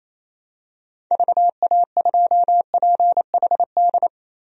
Morse code, heard as a keyed tone, about 28 wpm, 700 Hz. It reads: VA2PHD
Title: DXCC-100 Call Signs - 6 Characters - Repeated 2x - 28wpm